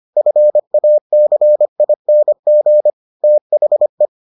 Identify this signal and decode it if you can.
Morse code, heard as a keyed tone, about 25 wpm, 600 Hz. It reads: FACING THE